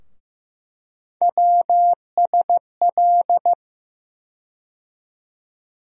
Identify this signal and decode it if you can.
Morse code, heard as a keyed tone, about 15 wpm, 700 Hz. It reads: WSL
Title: USA Call Sign Postfixes - 3 Letters 15wpm